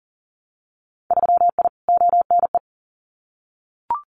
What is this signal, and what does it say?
Morse code, heard as a keyed tone, about 41 wpm, 700 Hz. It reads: 3I ONE
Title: Winter Field Day Exchanges Encore 40wpm